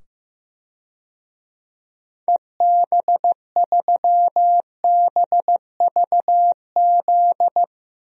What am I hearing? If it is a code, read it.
Morse code, heard as a keyed tone, about 15 wpm, 700 Hz. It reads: EB3BVZ
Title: DXCC-100 Call Signs - 6 Characters - 15wpm